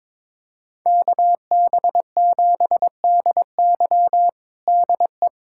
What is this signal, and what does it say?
Morse code, heard as a keyed tone, about 22 wpm, 700 Hz. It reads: KB7DY DE
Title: USA Callsign DE Callsign - 5 Characters - 22wpm